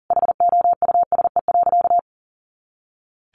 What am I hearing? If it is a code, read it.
Morse code, heard as a keyed tone, about 40 wpm, 700 Hz. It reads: HOUSE.
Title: Sentences from Top 400 Words 40wpm